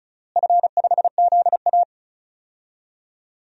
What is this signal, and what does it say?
Morse code, heard as a keyed tone, about 34 wpm, 700 Hz. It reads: F5ZA
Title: DXCC-100 Call Signs - 4 Characters - Repeated 3x - 35wpm